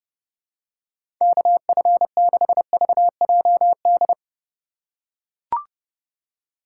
Morse code, heard as a keyed tone, about 30 wpm, 700 Hz. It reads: KF6VJD
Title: USA Call Signs - 6 Characters - 4x Character Spacing - 30wpm